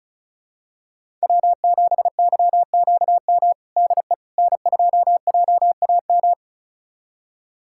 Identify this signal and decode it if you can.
Morse code, heard as a keyed tone, about 34 wpm, 700 Hz. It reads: W7YQM DE N2JAM
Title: USA Callsign DE Callsign - 5 Characters - 35wpm